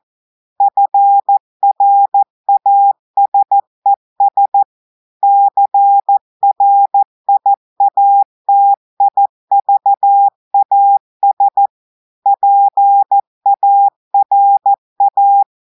Morse code, heard as a keyed tone, about 14 wpm, 800 Hz. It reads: FRASES CRIATIVAS PARA
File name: audios / portugais / portugais texte en clair 2 vitesse 14 mots minutes